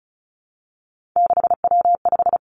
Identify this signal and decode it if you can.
Morse code, heard as a keyed tone, about 36 wpm, 700 Hz. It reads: 6W5
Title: Three Challenging Letter-Number Combinations 35wpm